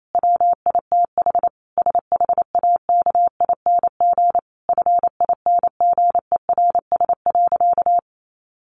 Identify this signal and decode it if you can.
Morse code, heard as a keyed tone, about 28 wpm, 700 Hz. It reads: WITH SHAKING FINGERS.
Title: A Sound Of Thunder 28wpm